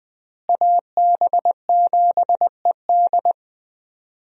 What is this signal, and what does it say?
Morse code, heard as a keyed tone, about 21 wpm, 700 Hz. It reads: AB7ED
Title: USA Call Signs - 5 Characters - Repeated 3x - 20wpm